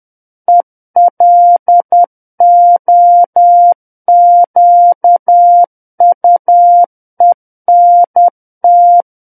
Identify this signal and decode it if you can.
Morse code, heard as a keyed tone, about 10 wpm, 700 Hz. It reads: ELOQUENT
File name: Book One / 01-Chapter 1 - 10 WPM